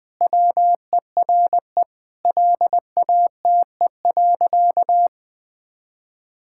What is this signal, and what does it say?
Morse code, heard as a keyed tone, about 20 wpm, 700 Hz. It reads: WERE LATE.
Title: Sentences with 5-Word Phrases - Spoken Beginning - 20wpm